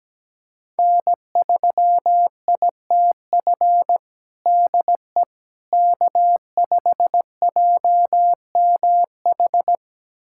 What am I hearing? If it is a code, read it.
Morse code, heard as a keyed tone, about 17 wpm, 700 Hz. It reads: N3ITF DE K5JMH
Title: USA Callsign DE Callsign - 5 Characters - 17wpm